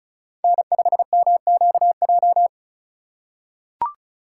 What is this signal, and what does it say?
Morse code, heard as a keyed tone, about 34 wpm, 700 Hz. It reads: N5MQJ
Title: USA Call Signs - 5 Characters - 3x Character Spacing - 35wpm